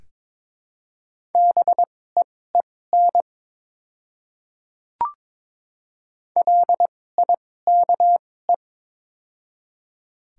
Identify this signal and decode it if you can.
Morse code, heard as a keyed tone, about 21 wpm, spaced as for 16 wpm, 700 Hz. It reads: BEEN LIKE
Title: 2x Character Spacing - Top 100 Words - 22wpm